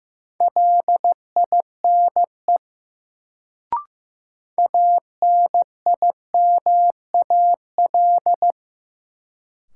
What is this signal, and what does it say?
Morse code, heard as a keyed tone, about 15 wpm, 700 Hz. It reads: LINE ANIMAL
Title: Sentences from Top 1000 Words - Review Part 1 - New Word Drill - 15wpm